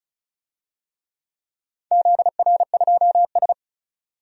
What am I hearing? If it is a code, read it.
Morse code, heard as a keyed tone, about 34 wpm, 700 Hz. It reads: ZR2S
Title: DXCC-100 Call Signs - 4 Characters - 35wpm